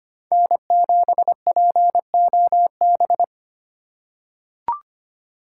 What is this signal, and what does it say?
Morse code, heard as a keyed tone, about 25 wpm, 700 Hz. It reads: N7POB
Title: USA Call Signs - 8x Character Spacing - 25wpm